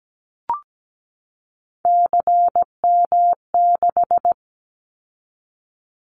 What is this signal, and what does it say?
Morse code, heard as a keyed tone, about 17 wpm, 700 Hz. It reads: CM6
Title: Callsign Prefixes for Top 100 DXCC - 17wpm